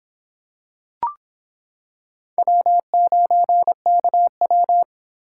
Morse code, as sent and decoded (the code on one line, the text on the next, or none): .-- ----. -.- .--
W9KW